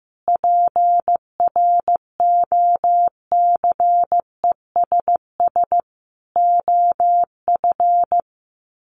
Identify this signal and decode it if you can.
Morse code, heard as a keyed tone, about 14 wpm, 700 Hz. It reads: PROCESS OF
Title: Sets of 5 Words Encore2 15wpm